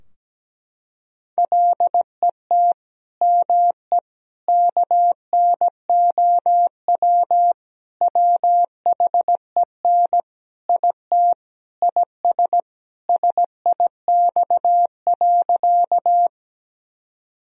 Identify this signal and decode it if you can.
Morse code, heard as a keyed tone, about 17 wpm, 700 Hz. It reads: LET ME KNOW WHEN IT IS SIX.
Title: Sentences from Top 1000 Words - Review Part 2 - 17wpm